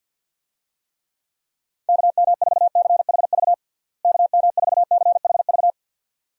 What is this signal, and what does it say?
Morse code, heard as a keyed tone, about 52 wpm, 700 Hz. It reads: KM4CHV KM4CHV
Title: USA Call Signs - 6 Characters - Repeated 2x - 50wpm